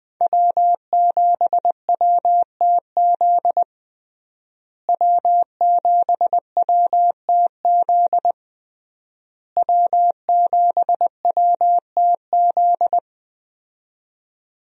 Morse code, heard as a keyed tone, about 21 wpm, 700 Hz. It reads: W7WTZ W7WTZ W7WTZ
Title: USA Call Signs - 5 Characters - Repeated 3x - 20wpm